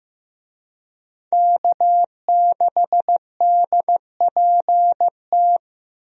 Morse code, heard as a keyed tone, about 15 wpm, 700 Hz. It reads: K6DPT
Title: USA Call Signs - 5 Characters - Repeated 2x - 15wpm